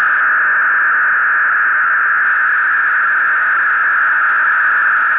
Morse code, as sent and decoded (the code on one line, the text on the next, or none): none